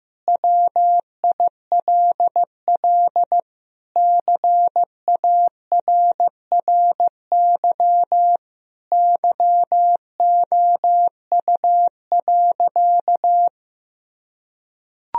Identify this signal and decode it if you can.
Morse code, heard as a keyed tone, about 15 wpm, 700 Hz. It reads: WILL CARRY YOU.
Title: Sentences from Top 1000 Words - Review Part 1 - 15wpm